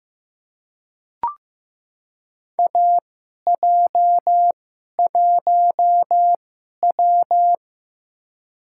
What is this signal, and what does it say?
Morse code, heard as a keyed tone, about 15 wpm, spaced as for 11 wpm, 700 Hz. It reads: AJ1W
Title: USA Call Signs - 4 Characters - 2x Character Spacing - 15wpm